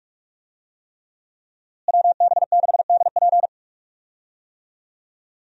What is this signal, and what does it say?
Morse code, heard as a keyed tone, about 44 wpm, 700 Hz. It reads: WB6DP
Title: USA Call Signs - 5 Characters - Repeated 3x - 45wpm